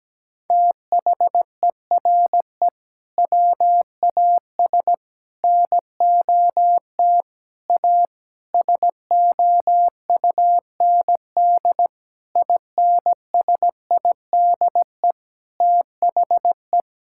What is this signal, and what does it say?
Morse code, heard as a keyed tone, about 17 wpm, 700 Hz. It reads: THERE WAS NOT A SOUND INSIDE THE
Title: Sentences from Top 1000 Words - Review Part 6 - 17wpm